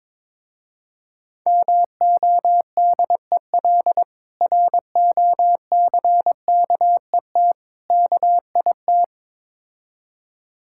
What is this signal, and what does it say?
Morse code, heard as a keyed tone, about 23 wpm, 690 Hz. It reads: MODEL ROCKET KIT
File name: Difficult Words with Hints - 22wpm